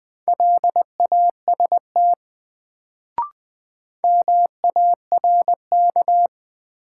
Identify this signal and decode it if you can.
Morse code, heard as a keyed tone, about 20 wpm, 700 Hz. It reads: LAST MARK
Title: Sentences from Top 300 Words - New Word Drill - 20wpm